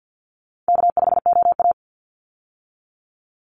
Morse code, heard as a keyed tone, about 50 wpm, 700 Hz. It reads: K5OA
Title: Easy Call Signs - 50wpm